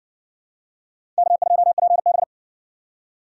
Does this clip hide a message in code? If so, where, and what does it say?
Yes: Morse code from 1.2 to 2.2 s, heard as a keyed tone, about 60 wpm, 700 Hz. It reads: K2PB